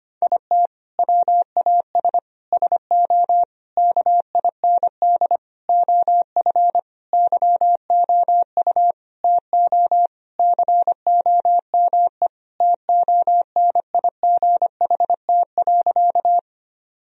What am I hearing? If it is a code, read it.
Morse code, heard as a keyed tone, about 25 wpm, 700 Hz. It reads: IT WAS SO KIND OF YOU TO COME TONIGHT.
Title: Sentences from Top 1000 Words - Review Part 3 - 25wpm